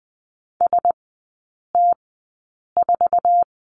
Morse code, heard as a keyed tone, about 20 wpm, spaced as for 8 wpm, 700 Hz. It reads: ST4
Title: 067 Lesson 12e - Letter d - Three Characters 20wpm